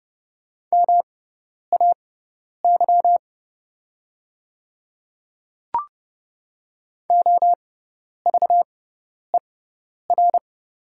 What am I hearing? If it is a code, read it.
Morse code, heard as a keyed tone, about 30 wpm, spaced as for 10 wpm, 700 Hz. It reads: MAY OVER
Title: 6x Character Spacing - Top 100 Words - 30wpm